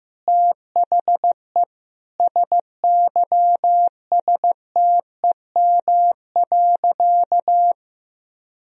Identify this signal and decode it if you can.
Morse code, heard as a keyed tone, about 15 wpm, 700 Hz. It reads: THE SYSTEM.
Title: Sentences from Top 1000 Words - Review Part 3 - 15wpm